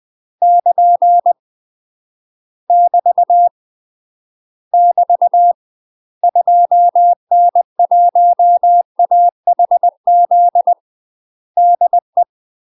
Morse code, heard as a keyed tone, about 20 wpm, 700 Hz. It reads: <KN> <BT> <BT> 2N1AHZ DE
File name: UK MASTER QSOs TRAINING / 20WPM / UK MASTER QSOs 0069